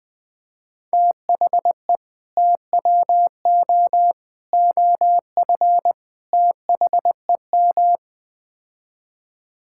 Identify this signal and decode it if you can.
Morse code, heard as a keyed tone, about 21 wpm, 700 Hz. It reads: THE TWO OF THEM
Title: Build A Phrase - 4-Word Phrases - 20wpm